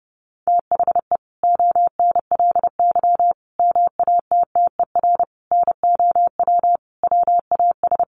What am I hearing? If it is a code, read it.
Morse code, heard as a keyed tone, about 30 wpm, 700 Hz. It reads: THE ONLY MATTER NOW WAS
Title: Sentences from Top 1400 Words 30wpm